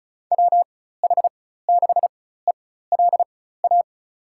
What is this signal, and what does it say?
Morse code, heard as a keyed tone, about 34 wpm, spaced as for 16 wpm, 700 Hz. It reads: WH6ELA
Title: USA Call Signs - 6 Characters - 4x Character Spacing - 35wpm